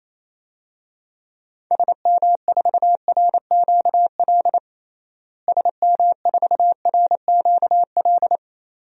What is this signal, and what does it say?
Morse code, heard as a keyed tone, about 28 wpm, 700 Hz. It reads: SM4RQL SM4RQL
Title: DXCC-100 Call Signs - 6 Characters - Repeated 2x - 28wpm